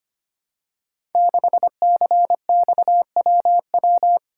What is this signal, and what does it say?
Morse code, heard as a keyed tone, about 25 wpm, 700 Hz. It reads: 6CXWW
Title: Code Groups - 5 Letter:Numbers - 25wpm